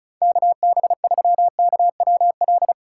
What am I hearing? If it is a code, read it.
Morse code, heard as a keyed tone, about 36 wpm, 700 Hz. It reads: KB3KWL
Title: USA Call Signs - 6 Characters - 3x Character Spacing - 35wpm